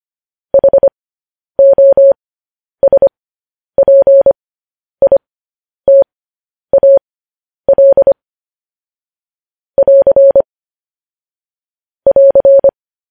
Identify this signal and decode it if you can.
Morse code, heard as a keyed tone, about 25 wpm, spaced as for 10 wpm, 550 Hz. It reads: HOSPITAL <AR> <AR>
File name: podcast / News headlines-1x-25x10